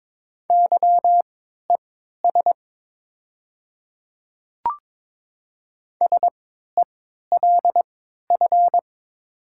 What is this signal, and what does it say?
Morse code, heard as a keyed tone, about 22 wpm, spaced as for 13 wpm, 700 Hz. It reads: YES SELF
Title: 3x Character Spacing - Top 500 Words - 22wpm